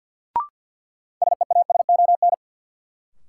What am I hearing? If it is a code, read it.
Morse code, heard as a keyed tone, about 51 wpm, 700 Hz. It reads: SEASON
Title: Sets of 5 Words - New Word Drill - 50wpm